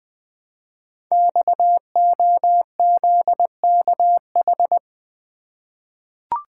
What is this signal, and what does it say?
Morse code, heard as a keyed tone, about 20 wpm, 700 Hz. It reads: XOZKH
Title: Code Groups - 5 Letter:Numbers - 20wpm